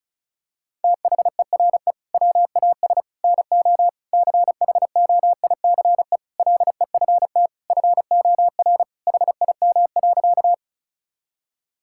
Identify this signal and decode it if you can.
Morse code, heard as a keyed tone, about 34 wpm, 700 Hz. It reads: THERE WAS NO CHOICE LEFT FOR HIM.
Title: Sentences from Top 1000 Words - Review Part 6 - 35wpm